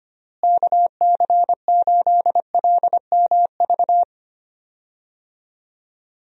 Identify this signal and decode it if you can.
Morse code, heard as a keyed tone, about 25 wpm, 700 Hz. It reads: KC8LMV
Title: USA Call Signs - 6 Characters - Repeated 2x - 25wpm